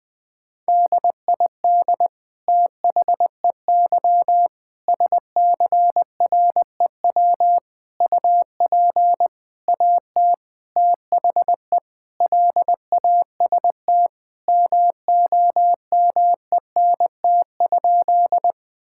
Morse code, heard as a keyed tone, about 21 wpm, 700 Hz. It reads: DID THEY SCREW UP AT THE LAST MOMENT?
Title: Sentences from Top 1600 Words 20wpm